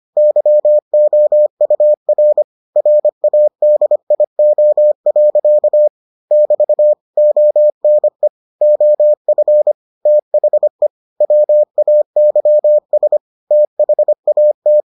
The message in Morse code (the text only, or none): YOUR RADIO. <BT> ONE OF THE WAYS THAT